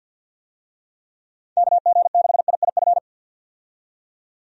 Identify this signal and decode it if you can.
Morse code, heard as a keyed tone, about 50 wpm, 700 Hz. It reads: KG6IIF
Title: USA Call Signs - 6 Characters - 50wpm